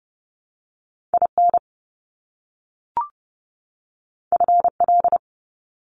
Morse code, heard as a keyed tone, about 30 wpm, 700 Hz. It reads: IN FL
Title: US States 30wpm